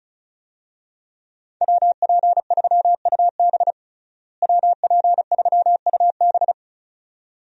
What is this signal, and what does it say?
Morse code, heard as a keyed tone, about 34 wpm, 700 Hz. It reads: WP3UB WP3UB
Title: USA Call Signs - 5 Characters - Repeated 2x - 35wpm